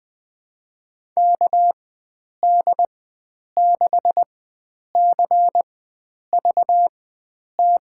Morse code, heard as a keyed tone, about 20 wpm, spaced as for 9 wpm, 700 Hz. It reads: KD6CVT
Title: USA Call Signs - 6 Characters - 4x Character Spacing - 20wpm